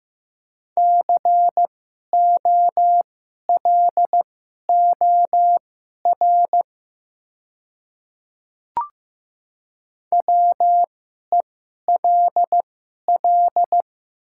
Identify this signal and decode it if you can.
Morse code, heard as a keyed tone, about 15 wpm, spaced as for 11 wpm, 700 Hz. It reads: COLOR WELL
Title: 2x Character Spacing - Top 500 Words - 15wpm